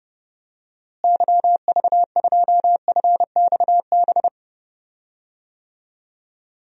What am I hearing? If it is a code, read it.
Morse code, heard as a keyed tone, about 30 wpm, 700 Hz. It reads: YV2FXB
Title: DXCC-100 Call Signs - 6 Characters - Repeated 3x - 30wpm